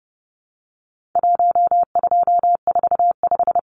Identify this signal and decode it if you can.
Morse code, heard as a keyed tone, about 30 wpm, 700 Hz. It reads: 1245
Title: Numbers - Contest Style - 4-Digit 30wpm